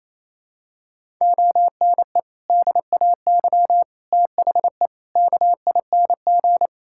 Morse code, heard as a keyed tone, about 28 wpm, 700 Hz. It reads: ONE DAY THE KING
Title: Sentences from Top 1600 Words 28wpm